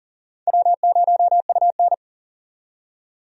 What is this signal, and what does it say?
Morse code, heard as a keyed tone, about 40 wpm, 700 Hz. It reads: W0UN
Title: USA Call Signs - 4 Characters - 40wpm